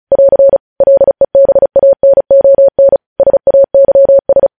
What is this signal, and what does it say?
Morse code, heard as a keyed tone, about 36 wpm, 550 Hz. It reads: <AR> LEBANON SAYS